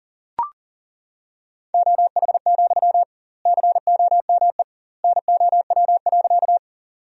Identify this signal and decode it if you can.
Morse code, heard as a keyed tone, about 37 wpm, 700 Hz. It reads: OH, COME NOW.
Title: 2x Word-Spacing - Sentences from Top 500 Words - 40wpm